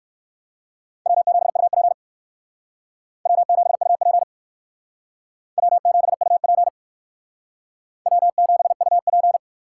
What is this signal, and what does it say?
Morse code, heard as a keyed tone, about 59 wpm, 700 Hz. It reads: W7UP W7UP W7UP W7UP